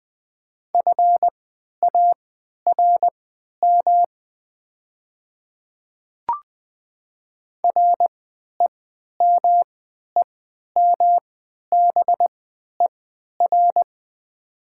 Morse code, heard as a keyed tone, about 20 wpm, spaced as for 11 wpm, 700 Hz. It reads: FARM REMEMBER
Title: 3x Character Spacing - Top 500 Words - 20wpm